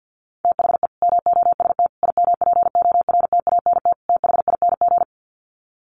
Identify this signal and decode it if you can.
Morse code, heard as a keyed tone, about 50 wpm, 700 Hz. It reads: THE MOST IMPORTANT THING
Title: Sets of 4 Words 50wpm